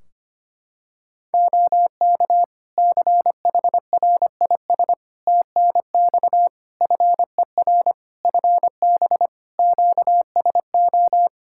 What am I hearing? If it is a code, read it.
Morse code, heard as a keyed tone, about 25 wpm, 700 Hz. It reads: OK CHRIS TNX FER FB QSO
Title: Practice QSO - Exchange 4 - Ending - 25wpm